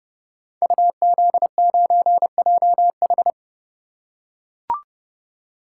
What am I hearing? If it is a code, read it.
Morse code, heard as a keyed tone, about 29 wpm, 700 Hz. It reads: UZ9JH
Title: Code Groups - 5 Letter:Numbers - 30wpm